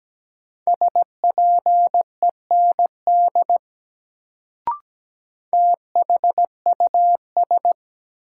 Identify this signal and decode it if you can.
Morse code, heard as a keyed tone, about 17 wpm, 700 Hz. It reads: SPEND THUS
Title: Sentences from Top 800 Words - New Word Drill - 17wpm